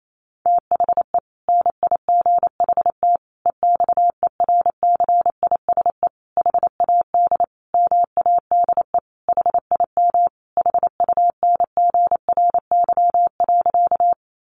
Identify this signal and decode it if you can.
Morse code, heard as a keyed tone, about 28 wpm, 700 Hz. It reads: THE NIGHT EXERCISE HAD MADE HIM HUNGRY.
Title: Sentences from Top 1400 Words 28wpm